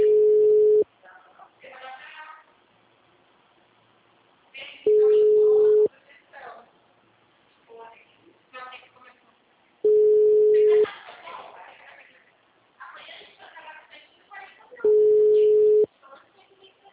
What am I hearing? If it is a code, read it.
Ringback tone (Europe, 425 Hz), with speech.